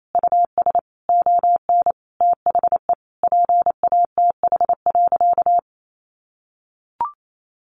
Morse code, heard as a keyed tone, about 28 wpm, 700 Hz. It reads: US ON THE PATH.
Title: Sentences from Top 1400 Words 28wpm